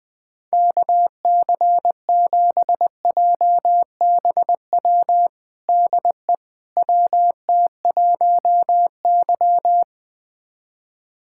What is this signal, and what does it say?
Morse code, heard as a keyed tone, about 20 wpm, 700 Hz. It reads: KC7JBW DE WT1Y
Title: USA Callsign DE Callsign 20wpm